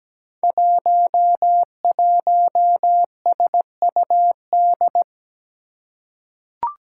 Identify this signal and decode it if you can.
Morse code, heard as a keyed tone, about 17 wpm, 700 Hz. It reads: 11SUD
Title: Five Challenging Letter-Number Combinations 17wpm